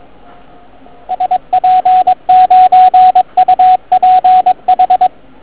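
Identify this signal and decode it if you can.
Morse code, heard as a keyed tone, about 22 wpm, 750 Hz, with radio noise. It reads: SP9UPH
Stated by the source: Morning 80m pile-up recorded at nearly the same time on three transceivers: FT1000MKV, FT817 and ATS-3B.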